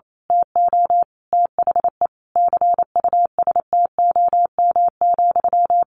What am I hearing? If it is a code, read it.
Morse code, heard as a keyed tone, about 28 wpm, 700 Hz. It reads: TO THE CUSTOM,
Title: A Sound Of Thunder 28wpm